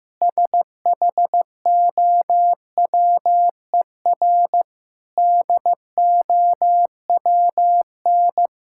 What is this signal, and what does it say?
Morse code, heard as a keyed tone, about 16 wpm, 700 Hz. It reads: SHOWER DOWN
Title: Sentences from Top 1600 Words 15wpm